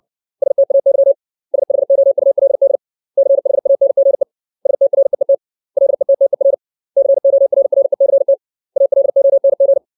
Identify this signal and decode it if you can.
Morse code, heard as a keyed tone, about 60 wpm, 550 Hz. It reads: STAY SHOULD CHANGE STREET LETTER CORRECT ALONG
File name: 632 words / 632 Words 60WPM